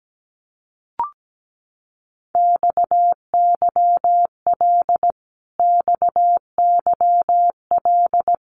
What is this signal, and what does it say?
Morse code, heard as a keyed tone, about 17 wpm, 700 Hz. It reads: XYL XYL
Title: Abbreviations Repeated 3x - 17wpm